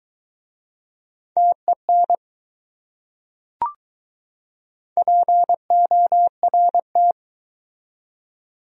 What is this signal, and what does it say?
Morse code, heard as a keyed tone, about 21 wpm, 700 Hz. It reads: TEN PORT